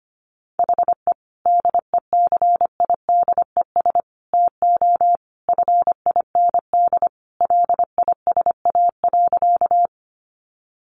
Morse code, heard as a keyed tone, about 25 wpm, 700 Hz. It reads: HE DECIDES TO FIND LISA.
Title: Additional Easy Phrases 25wpm